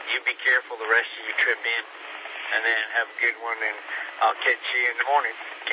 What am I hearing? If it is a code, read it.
Emergency Alert System (SAME) tones, heard as harsh buzzing bursts, with speech. It is heard from 1.2 to 5.0 s.